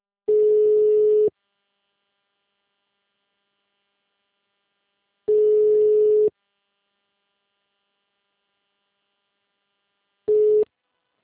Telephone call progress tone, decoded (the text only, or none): ringback tone